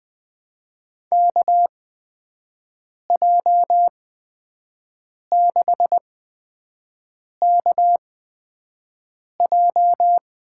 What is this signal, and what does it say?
Morse code, heard as a keyed tone, about 21 wpm, spaced as for 5 wpm, 700 Hz. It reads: KJ6KJ
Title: USA Call Signs - 8x Character Spacing - 20wpm